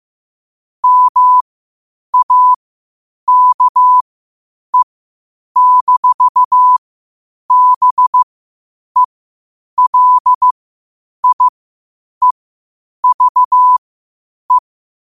Morse code, heard as a keyed tone, about 15 wpm, spaced as for 8 wpm, 1000 Hz. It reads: MAKE-BELIEVE
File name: qotdfiles / 8wpm